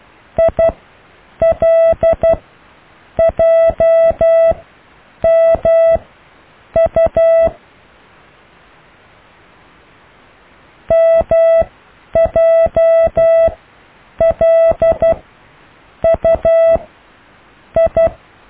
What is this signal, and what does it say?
Morse code, heard as a keyed tone, about 12 wpm, spaced as for 7 wpm, 660 Hz. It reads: ILJMU MJLUI
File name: cw / IJLMU